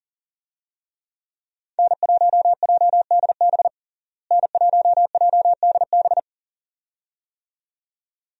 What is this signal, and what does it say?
Morse code, heard as a keyed tone, about 40 wpm, 700 Hz. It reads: N1JDB N1JDB
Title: USA Call Signs - 5 Characters - Repeated 2x - 40wpm